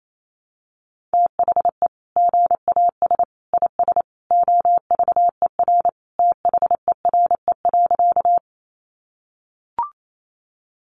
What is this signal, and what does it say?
Morse code, heard as a keyed tone, about 28 wpm, 700 Hz. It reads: THE GAS IS OVER THERE.
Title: Sentences from Top 1100 Words 28wpm